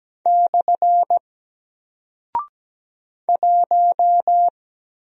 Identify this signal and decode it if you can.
Morse code, heard as a keyed tone, about 17 wpm, 700 Hz. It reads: / 1
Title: Single Character (Letter:Number:Punctuation:Prosign) 17wpm